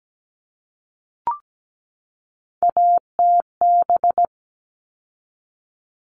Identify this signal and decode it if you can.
Morse code, heard as a keyed tone, about 17 wpm, 700 Hz. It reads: ATB
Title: All Cut Numbers - 3-Digit 17wpm